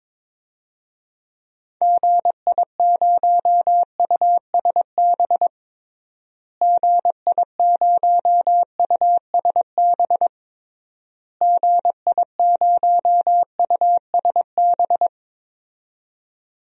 Morse code, heard as a keyed tone, about 22 wpm, 700 Hz. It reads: GI0USB GI0USB GI0USB
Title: DXCC-100 Call Signs - 6 Characters - Repeated 3x - 22wpm